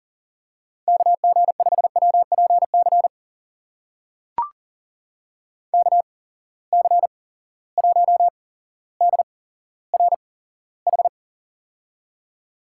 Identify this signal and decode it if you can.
Morse code, heard as a keyed tone, about 40 wpm, spaced as for 17 wpm, 700 Hz. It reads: KG5WPC KC1DRH
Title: USA Call Signs - 6 Characters - 8x Character Spacing - 40wpm